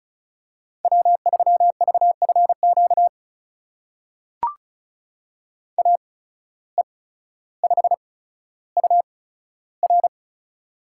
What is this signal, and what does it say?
Morse code, heard as a keyed tone, about 36 wpm, spaced as for 15 wpm, 700 Hz. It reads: W3VFQ AE5UR
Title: USA Call Signs - 5 Characters - 8x Character Spacing - 35wpm